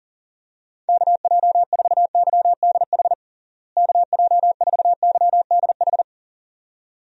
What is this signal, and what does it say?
Morse code, heard as a keyed tone, about 40 wpm, 700 Hz. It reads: KJ4YDH KJ4YDH
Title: USA Call Signs - 6 Characters - Repeated 2x - 40wpm